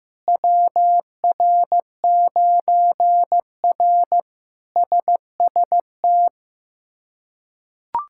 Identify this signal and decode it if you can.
Morse code, heard as a keyed tone, about 15 wpm, 700 Hz. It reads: WR9R SST
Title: K1USN SST QSO Practice - 15wpm